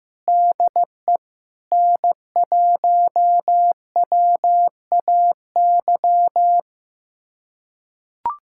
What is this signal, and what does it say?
Morse code, heard as a keyed tone, about 15 wpm, 700 Hz. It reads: DE N1WAY
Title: USA Callsign DE Callsign 15wpm